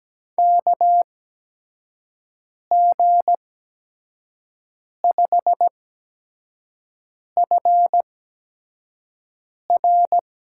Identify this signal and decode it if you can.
Morse code, heard as a keyed tone, about 17 wpm, spaced as for 5 wpm, 700 Hz. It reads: KG5FR
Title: USA Call Signs - 6 Characters - 8x Character Spacing - 17wpm